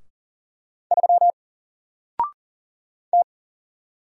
Morse code, heard as a keyed tone, about 40 wpm, 700 Hz. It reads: 3 T
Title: Single Letter-Number Alt 40wpm